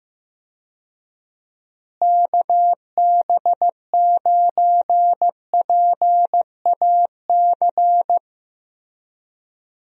Morse code, heard as a keyed tone, about 15 wpm, 700 Hz. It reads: KB9PAC
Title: USA Call Signs - 6 Characters - Repeated 2x - 15wpm